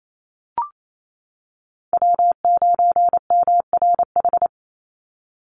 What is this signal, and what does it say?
Morse code, heard as a keyed tone, about 28 wpm, 700 Hz. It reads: W9MRH
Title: NA SOTA Chaser Callsigns 28wpm